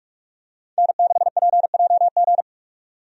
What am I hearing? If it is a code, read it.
Morse code, heard as a keyed tone, about 47 wpm, 700 Hz. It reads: N6PJG
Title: USA Call Signs - 5 Characters - 45wpm